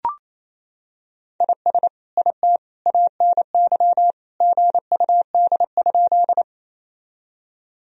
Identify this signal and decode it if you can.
Morse code, heard as a keyed tone, about 28 wpm, 700 Hz. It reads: IS IT ANY GUD?
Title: Sentences from Top 200 Words and Abbreviations 28wpm